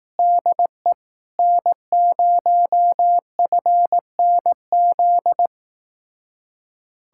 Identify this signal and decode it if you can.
Morse code, heard as a keyed tone, about 18 wpm, 700 Hz. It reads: DE N0FNZ
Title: Speed-Racing Common QSO Phrases Encore - 15wpm